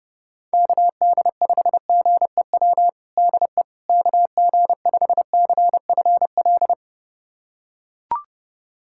Morse code, heard as a keyed tone, about 30 wpm, 700 Hz. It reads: KD5GEW DE KG5CFL
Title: USA Callsign DE Callsign - 6 Characters - 30wpm